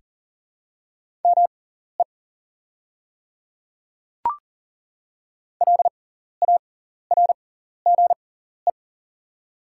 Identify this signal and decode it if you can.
Morse code, heard as a keyed tone, about 39 wpm, spaced as for 14 wpm, 700 Hz. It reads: ME LARGE
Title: 6x Character Spacing - Top 300 Words - 40wpm